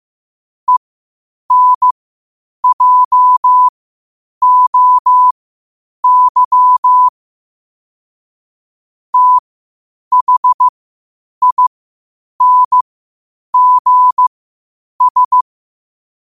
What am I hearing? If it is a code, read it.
Morse code, heard as a keyed tone, about 15 wpm, spaced as for 8 wpm, 1000 Hz. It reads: ENJOY THINGS
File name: qotdfiles / 8wpm